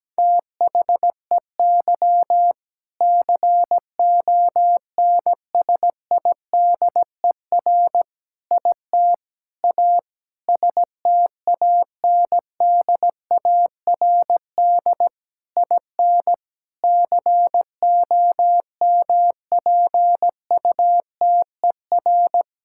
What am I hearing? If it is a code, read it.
Morse code, heard as a keyed tone, about 17 wpm, 700 Hz. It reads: THEY CONSIDER IT A STANDARD IN COMPUTER
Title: Sentences from Top 1000 Words - Review Part 6 - 17wpm